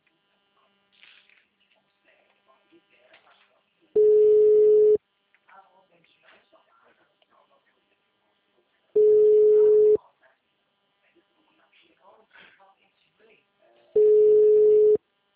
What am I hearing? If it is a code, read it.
Ringback tone (Europe, 425 Hz), with speech.